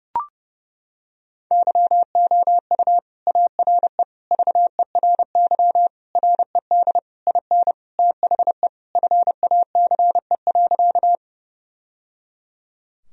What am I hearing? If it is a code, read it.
Morse code, heard as a keyed tone, about 32 wpm, 700 Hz. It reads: YOU ARE VERY RED IN THE FACE.
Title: Sentences from Top 1000 Words - Review Part 2 - 30wpm